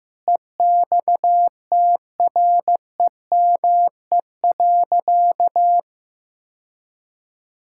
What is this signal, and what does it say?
Morse code, heard as a keyed tone, about 15 wpm, 700 Hz. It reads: EXTREME.
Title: Sentences from Top 1600 Words 15wpm